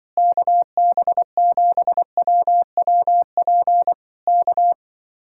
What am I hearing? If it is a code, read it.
Morse code, heard as a keyed tone, about 24 wpm, 700 Hz. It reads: KB7WWP K
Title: Speed-Racing Common QSO Phrases - 20wpm